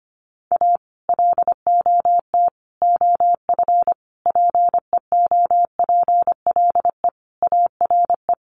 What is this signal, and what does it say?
Morse code, heard as a keyed tone, about 25 wpm, 700 Hz. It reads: A LOT OF PEOPLE ARE
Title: Sets of 5 Words 25wpm